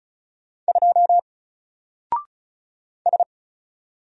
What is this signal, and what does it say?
Morse code, heard as a keyed tone, about 36 wpm, 700 Hz. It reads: 2 S
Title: Single Character (Letter:Number:Punctuation:Prosign) Alt 35wpm